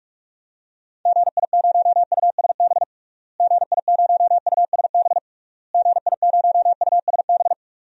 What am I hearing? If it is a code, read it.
Morse code, heard as a keyed tone, about 44 wpm, 700 Hz. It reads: GI0USB GI0USB GI0USB
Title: DXCC-100 Call Signs - 6 Characters - Repeated 3x - 45wpm